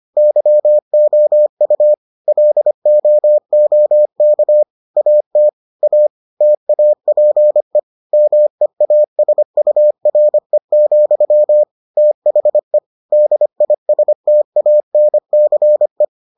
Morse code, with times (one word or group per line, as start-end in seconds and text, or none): 0.2-1.9 YOU
2.3-4.6 LOOK
5.0-5.5 AT
5.8-6.1 A
6.4-7.8 TAPE
8.1-11.6 MEASURE,
12.0-12.8 THE
13.1-16.0 DISTANCE